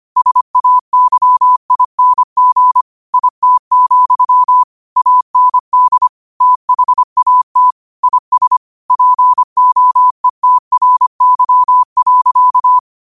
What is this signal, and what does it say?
Morse code, heard as a keyed tone, about 25 wpm, 1000 Hz. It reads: SAYING IT, AND THAT IS POETRY.
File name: qotdfiles / 25wpm